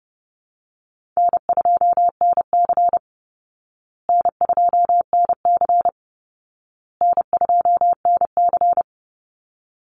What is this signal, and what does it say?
Morse code, heard as a keyed tone, about 30 wpm, 700 Hz. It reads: N2NC N2NC N2NC
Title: Easy Call Signs Repeated 3x - 30wpm